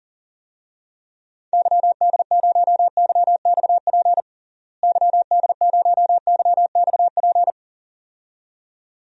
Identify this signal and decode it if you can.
Morse code, heard as a keyed tone, about 40 wpm, 700 Hz. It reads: YD0YXP YD0YXP
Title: DXCC-100 Call Signs - 6 Characters - Repeated 2x - 40wpm